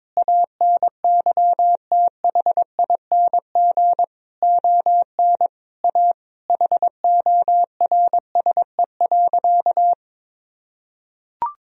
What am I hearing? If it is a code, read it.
Morse code, heard as a keyed tone, about 22 wpm, 700 Hz. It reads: ANYTHING ON A HORSE.
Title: Sentences from Top 1000 Words - Review Part 1 - 22wpm